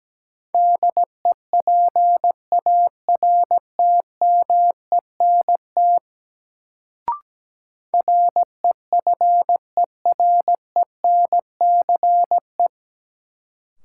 Morse code, heard as a keyed tone, about 17 wpm, 700 Hz. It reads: DEPARTMENT REFERENCE
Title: Sentences from Top 1200 Words - New Word Drill Alt - 17wpm